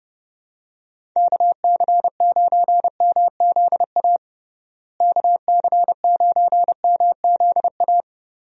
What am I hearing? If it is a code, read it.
Morse code, heard as a keyed tone, about 30 wpm, 700 Hz. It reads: KC9MZA KC9MZA
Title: USA Call Signs - 6 Characters - Repeated 3x - 30wpm